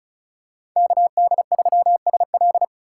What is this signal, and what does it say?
Morse code, heard as a keyed tone, about 36 wpm, 700 Hz. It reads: KD3SL
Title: USA Call Signs - 5 Characters - 3x Character Spacing - 35wpm